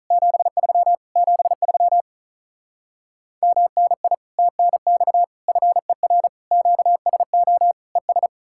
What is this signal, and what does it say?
Morse code, heard as a keyed tone, about 47 wpm, 700 Hz. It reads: 73 73 MNI TNX FER QSO ES